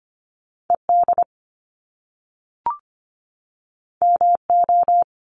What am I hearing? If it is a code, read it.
Morse code, heard as a keyed tone, about 25 wpm, 700 Hz. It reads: ED MO
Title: Top Two Letter Combinations in Words 25wpm